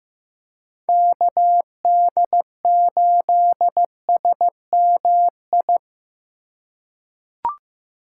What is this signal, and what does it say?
Morse code, heard as a keyed tone, about 16 wpm, 700 Hz. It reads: KD8SMI
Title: USA Call Signs - 6 Characters - 4x Character Spacing - 15wpm